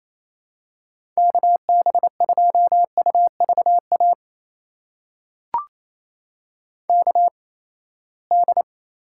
Morse code, heard as a keyed tone, about 29 wpm, spaced as for 20 wpm, 700 Hz. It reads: KB2UVA KD
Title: USA Call Signs - 8x Character Spacing - 28wpm